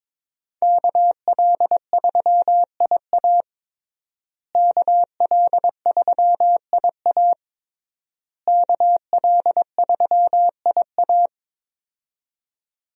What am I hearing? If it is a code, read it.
Morse code, heard as a keyed tone, about 22 wpm, 700 Hz. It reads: KL3IA KL3IA KL3IA
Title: USA Call Signs - 5 Characters - Repeated 3x - 22wpm